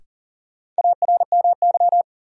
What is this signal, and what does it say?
Morse code, heard as a keyed tone, about 40 wpm, 700 Hz. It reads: ARMY
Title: Sentences from Top 800 Words - New Word Drill - 40wpm